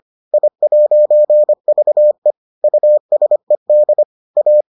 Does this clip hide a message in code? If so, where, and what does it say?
Yes: Morse code from 0.3 to 4.6 s, heard as a keyed tone, about 25 wpm, 600 Hz. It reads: I'VE USED A